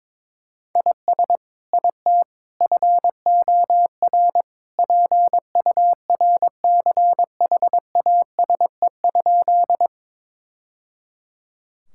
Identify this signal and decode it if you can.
Morse code, heard as a keyed tone, about 22 wpm, 700 Hz. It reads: IS IT FOR PURCHASE?
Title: Sentences from Top 1000 Words - Review Part 5 - 22wpm